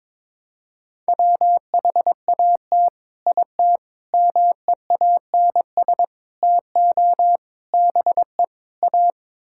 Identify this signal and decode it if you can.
Morse code, heard as a keyed tone, about 22 wpm, 700 Hz. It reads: WHAT IT MEANS TO BE A
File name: practice / 22wpm / Sets of 6 Words - 22wpm